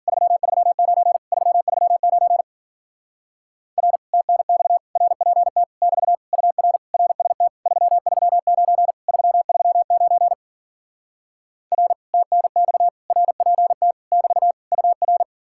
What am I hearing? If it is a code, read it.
Morse code, heard as a keyed tone, about 52 wpm, 700 Hz. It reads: 339 339 R TNX RPT <BT> UR RST 339 339 R TNX RPT <BT> UR